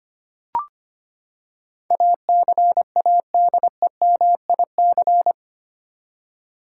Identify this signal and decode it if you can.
Morse code, heard as a keyed tone, about 25 wpm, 700 Hz. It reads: ACADEMIC
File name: Eight Letter Words - 25wpm